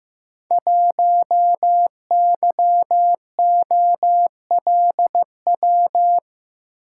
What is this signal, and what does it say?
Morse code, heard as a keyed tone, about 15 wpm, 700 Hz. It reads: 1YOLW
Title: Code Groups - 5 Letter:Numbers - 15wpm